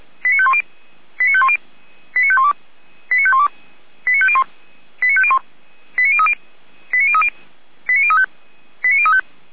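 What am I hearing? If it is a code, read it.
Five-tone selective call (ZVEI), heard as quick runs of five tones, tones of about 70 ms. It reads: 89510 89510 89512 89512 89501 89501 89030 89030 89035 89035
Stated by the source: ZVEI five-tone selective call, address 89062, received by SDR.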